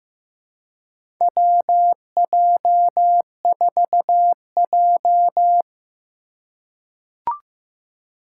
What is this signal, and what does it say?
Morse code, heard as a keyed tone, about 15 wpm, 700 Hz. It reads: WJ4J
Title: USA Call Signs - 4 Characters - 4x Character Spacing - 15wpm